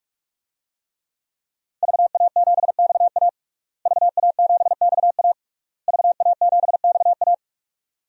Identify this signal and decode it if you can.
Morse code, heard as a keyed tone, about 46 wpm, 700 Hz. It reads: VA7XA VA7XA VA7XA
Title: DXCC-100 Call Signs - 5 Characters - Repeated 3x - 45wpm